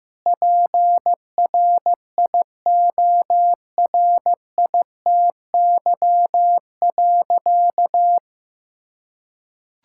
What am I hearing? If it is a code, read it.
Morse code, heard as a keyed tone, about 15 wpm, 700 Hz. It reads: PRIORITY.
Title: Sentences from Top 1600 Words 15wpm